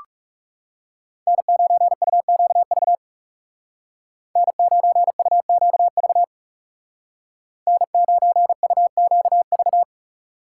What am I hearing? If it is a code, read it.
Morse code, heard as a keyed tone, about 38 wpm, 700 Hz. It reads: N9UQV N9UQV N9UQV